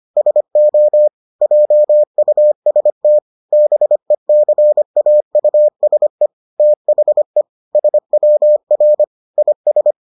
Morse code, heard as a keyed tone, about 25 wpm, 600 Hz. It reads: SO JUST BECAUSE THE SWR IS